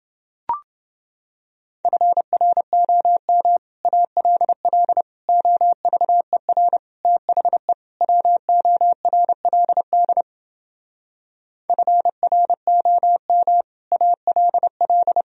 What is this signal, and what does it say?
Morse code, heard as a keyed tone, about 29 wpm, 700 Hz. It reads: FROM ALL OVER THE WORLD FROM ALL